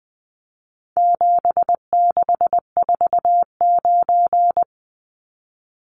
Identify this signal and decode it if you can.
Morse code, heard as a keyed tone, about 20 wpm, 700 Hz. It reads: 7649
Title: Numbers - 4-Digit 20wpm